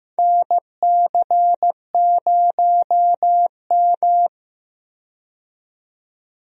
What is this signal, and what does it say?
Morse code, heard as a keyed tone, about 15 wpm, 700 Hz. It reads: NC0M
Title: USA Call Signs - 4 Characters - 15wpm